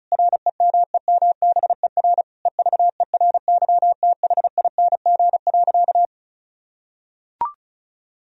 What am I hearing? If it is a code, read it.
Morse code, heard as a keyed tone, about 34 wpm, 700 Hz. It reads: REMEMBER EVERYTHING.
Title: Sentences from Top 1000 Words - Review Part 1 - 35wpm